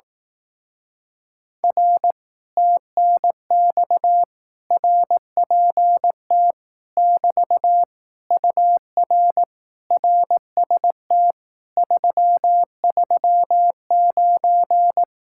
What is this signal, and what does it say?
Morse code, heard as a keyed tone, about 18 wpm, 700 Hz. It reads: R TNX RPT <BT> UR RST 339
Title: Speed-Racing Common QSO Phrases Encore - 15wpm